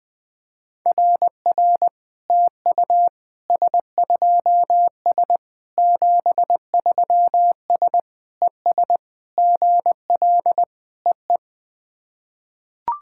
Morse code, heard as a keyed tone, about 20 wpm, 700 Hz. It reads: RR TU S2S 73S ES GL EE
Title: SOTA QSO Practice 20wpm